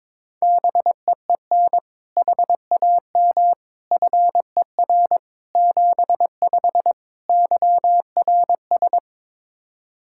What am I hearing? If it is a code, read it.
Morse code, heard as a keyed tone, about 22 wpm, 700 Hz. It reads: BEEN HAM FER 75 YRS
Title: Practice QSO - Exchange 3 - More Info - 22wpm